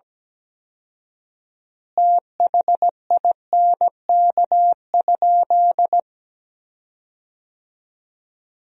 Morse code, heard as a keyed tone, about 17 wpm, 700 Hz. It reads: THINK?
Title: 4x Word-Spacing - Sentences from Top 200 Words - 17wpm